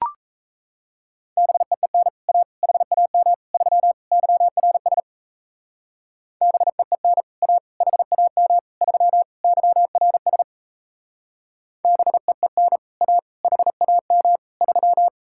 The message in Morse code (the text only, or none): BEEN A HAM 3 YRS BEEN A HAM 3 YRS BEEN A HAM 3